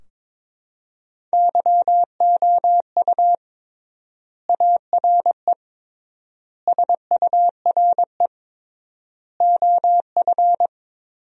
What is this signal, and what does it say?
Morse code, heard as a keyed tone, about 21 wpm, 700 Hz. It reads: YOU ARE SURE OF
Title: 3x Word-Spacing - Sentences from Top 300 Words - 22wpm